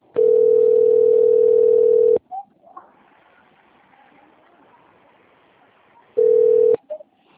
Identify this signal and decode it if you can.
Ringback tone (North America), 440 + 480 Hz.